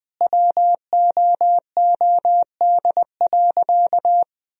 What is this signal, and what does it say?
Morse code, heard as a keyed tone, about 20 wpm, 700 Hz. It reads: WOOD.
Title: Sentences from Top 1000 Words - Review Part 3 - 20wpm